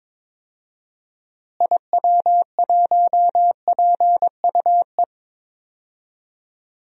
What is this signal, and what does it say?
Morse code, heard as a keyed tone, about 22 wpm, 700 Hz. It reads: IW1PUE
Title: DXCC-100 Call Signs - 6 Characters - Repeated 3x - 22wpm